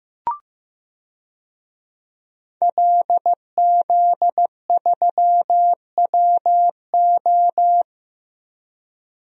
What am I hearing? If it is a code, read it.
Morse code, heard as a keyed tone, about 15 wpm, 700 Hz. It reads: LZ3WO
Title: DXCC-100 Call Signs - 5 Characters - Repeated 2x - 15wpm